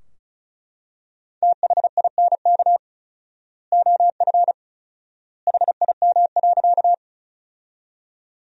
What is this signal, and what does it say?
Morse code, heard as a keyed tone, about 36 wpm, 700 Hz. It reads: THINK OF HIM.
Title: 4x Word-Spacing - Sentences from Top 200 Words - 35wpm